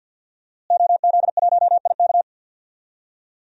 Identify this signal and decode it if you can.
Morse code, heard as a keyed tone, about 52 wpm, 700 Hz. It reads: OZ1IK
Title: DXCC-100 Call Signs - 5 Characters - Repeated 2x - 50wpm